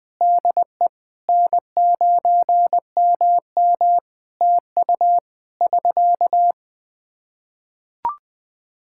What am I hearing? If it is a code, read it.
Morse code, heard as a keyed tone, about 20 wpm, 700 Hz. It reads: DE N9MM TU <SK>
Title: Practice QSO - Exchange 4 - Ending - 20wpm